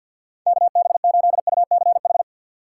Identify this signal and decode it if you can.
Morse code, heard as a keyed tone, about 52 wpm, 700 Hz. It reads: KB8UCH
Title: USA Call Signs - 6 Characters - 6x Character Spacing - 50wpm